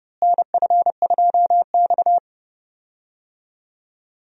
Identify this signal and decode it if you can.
Morse code, heard as a keyed tone, about 30 wpm, 700 Hz. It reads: NF2X
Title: USA Call Signs - 4 Characters - Repeated 2x - 30wpm